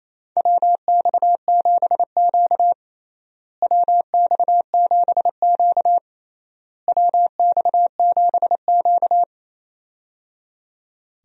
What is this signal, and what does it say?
Morse code, heard as a keyed tone, about 28 wpm, 700 Hz. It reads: WX7Q WX7Q WX7Q
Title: USA Call Signs - 4 Characters - Repeated 3x - 28wpm